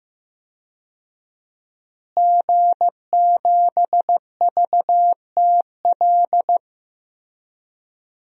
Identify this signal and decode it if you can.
Morse code, heard as a keyed tone, about 15 wpm, 700 Hz. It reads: G7VTL
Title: DXCC-100 Call Signs - 5 Characters - Repeated 3x - 15wpm